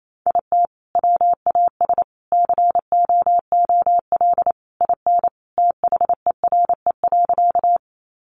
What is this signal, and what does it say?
Morse code, heard as a keyed tone, about 28 wpm, 700 Hz. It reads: IT WAS COOL IN THERE.
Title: Sentences from Top 1200 Words 28wpm